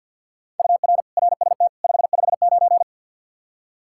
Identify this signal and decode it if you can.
Morse code, heard as a keyed tone, about 52 wpm, 700 Hz. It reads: UR RST 559
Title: Practice QSO - Exchange 1 - Beginning - 50wpm